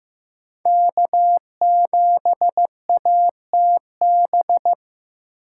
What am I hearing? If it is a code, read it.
Morse code, heard as a keyed tone, about 15 wpm, 700 Hz. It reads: K7ATB
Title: USA Call Signs - 6x Character Spacing - 15wpm